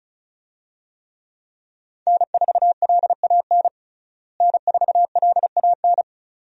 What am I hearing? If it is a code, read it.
Morse code, heard as a keyed tone, about 34 wpm, 700 Hz. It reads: N4LAN N4LAN
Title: USA Call Signs - 5 Characters - Repeated 2x - 35wpm